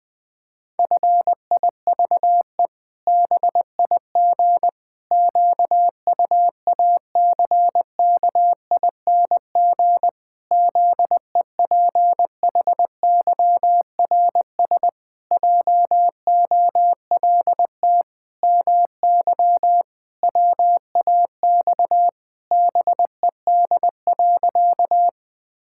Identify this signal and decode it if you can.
Morse code, heard as a keyed tone, about 20 wpm, 700 Hz. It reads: FIVE BIG QUACKING ZEPHYRS JOLT MY WAX BED.
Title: Pangram Challenge v2 - 20wpm